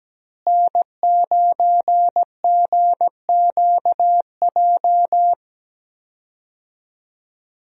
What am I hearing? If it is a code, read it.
Morse code, heard as a keyed tone, about 17 wpm, 700 Hz. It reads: N9GQJ
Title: USA Call Signs - 5 Characters - Repeated 3x - 17wpm